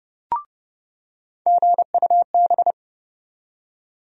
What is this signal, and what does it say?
Morse code, heard as a keyed tone, about 32 wpm, 700 Hz. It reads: GUB
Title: USA Call Sign Postfixes - 3 Letters 30wpm